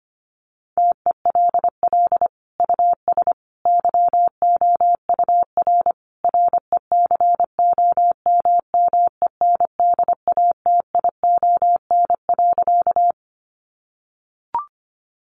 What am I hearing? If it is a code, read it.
Morse code, heard as a keyed tone, about 25 wpm, 700 Hz. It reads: TELL US YOUR RECOMMENDATION.
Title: Sentences from Top 1400 Words 25wpm